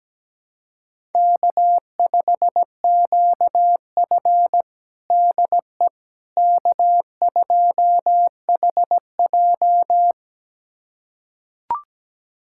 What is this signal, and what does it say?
Morse code, heard as a keyed tone, about 17 wpm, 700 Hz. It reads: K5QF DE K2HJ
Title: USA Callsign DE Callsign - 4 Characters - 17wpm